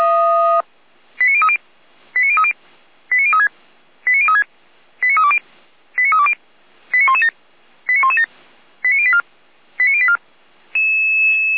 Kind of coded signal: five-tone selective call